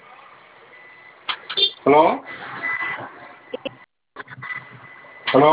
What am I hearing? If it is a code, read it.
Emergency Alert System (SAME) tones, heard as harsh buzzing bursts, with speech. It is heard from 0.6 to 4.7 s.